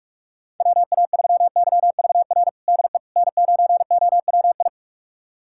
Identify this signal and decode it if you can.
Morse code, heard as a keyed tone, about 41 wpm, 700 Hz. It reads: WA3YVR DE N9OWI
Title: USA Callsign DE Callsign 45wpm